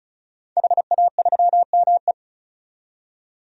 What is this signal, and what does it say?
Morse code, heard as a keyed tone, about 36 wpm, 700 Hz. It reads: HA3ME
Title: DXCC-100 Call Signs - 5 Characters - 35wpm